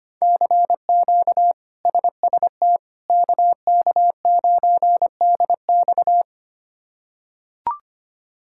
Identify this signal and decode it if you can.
Morse code, heard as a keyed tone, about 25 wpm, 700 Hz. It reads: CQ SST KK9DX
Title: K1USN SST QSO Practice - 25wpm